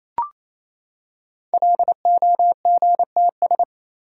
Morse code, heard as a keyed tone, about 29 wpm, 700 Hz. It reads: LOGTS
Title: Code Groups - 5 Letter:Numbers - 28wpm